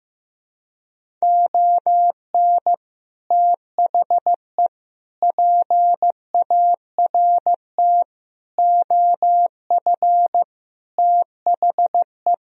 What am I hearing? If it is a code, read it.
Morse code, heard as a keyed tone, about 15 wpm, 700 Hz. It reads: ON THE PART OF THE
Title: Speed-Racing - Sets of 5 Words 15wpm